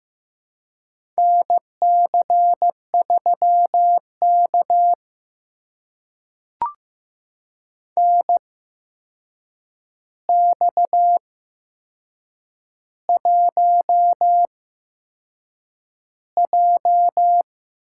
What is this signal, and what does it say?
Morse code, heard as a keyed tone, about 15 wpm, spaced as for 6 wpm, 700 Hz. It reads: NC3K NX1J
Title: USA Call Signs - 4 Characters - 8x Character Spacing - 15wpm